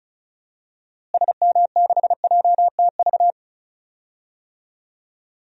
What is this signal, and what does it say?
Morse code, heard as a keyed tone, about 34 wpm, 700 Hz. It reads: SM6JTV
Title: DXCC-100 Call Signs - 6 Characters - Repeated 2x - 35wpm